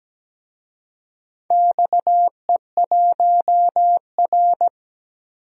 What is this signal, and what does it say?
Morse code, heard as a keyed tone, about 17 wpm, 700 Hz. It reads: XE1R
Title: DXCC-100 Call Signs - 4 Characters - 17wpm